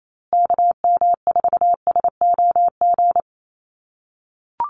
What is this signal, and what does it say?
Morse code, heard as a keyed tone, about 28 wpm, 700 Hz. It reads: KM4SOG
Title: USA Call Signs Repeated 3x 28wpm